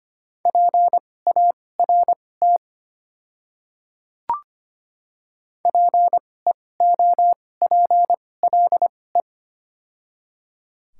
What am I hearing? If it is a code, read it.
Morse code, heard as a keyed tone, about 25 wpm, spaced as for 18 wpm, 690 Hz. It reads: PART PEOPLE
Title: 2x Character Spacing - Top 200 Words - 25wpm